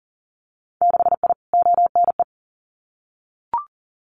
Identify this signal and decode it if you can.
Morse code, heard as a keyed tone, about 40 wpm, 700 Hz. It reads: 6I ONE
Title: Winter Field Day Exchanges Encore 40wpm